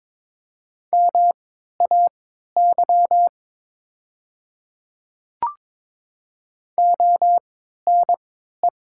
Morse code, heard as a keyed tone, about 22 wpm, spaced as for 12 wpm, 700 Hz. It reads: MAY ONE
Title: 3x Character Spacing - Top 100 Words - 22wpm